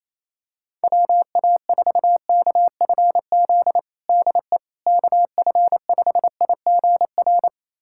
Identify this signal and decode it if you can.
Morse code, heard as a keyed tone, about 28 wpm, 700 Hz. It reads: WA4KFZ DE KF5IGR
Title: USA Callsign DE Callsign - 6 Characters - 28wpm